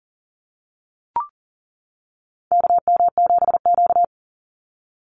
Morse code, heard as a keyed tone, about 40 wpm, 700 Hz. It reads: KM7Q
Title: Call Signs - 40wpm